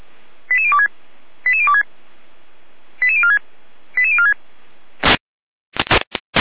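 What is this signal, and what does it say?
Five-tone selective call (ZVEI), heard as quick runs of five tones, tones of about 70 ms. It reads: 88026 88026 88046 88046
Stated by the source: Five-tone ZVEI selective call 88014, received by SDR.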